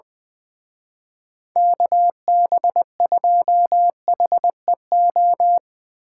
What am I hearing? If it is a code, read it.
Morse code, heard as a keyed tone, about 22 wpm, 700 Hz. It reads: KB2HEO
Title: Speed Racing - USA 2x3 Call Signs 20wpm